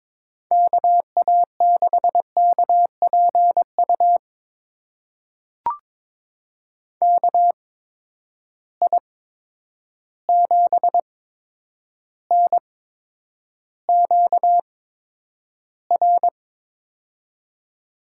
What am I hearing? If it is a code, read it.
Morse code, heard as a keyed tone, about 22 wpm, spaced as for 9 wpm, 700 Hz. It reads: KA6KPU KI7NQR
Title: USA Call Signs - 6 Characters - 8x Character Spacing - 22wpm